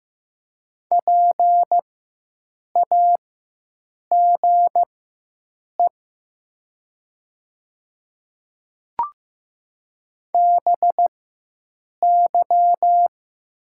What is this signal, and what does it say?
Morse code, heard as a keyed tone, about 15 wpm, spaced as for 7 wpm, 700 Hz. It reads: PAGE BY
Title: 4x Character Spacing - Top 300 Words - 15wpm